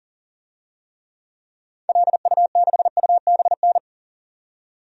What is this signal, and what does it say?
Morse code, heard as a keyed tone, about 38 wpm, 700 Hz. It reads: LU6UBN
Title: DXCC-100 Call Signs - 6 Characters - 40wpm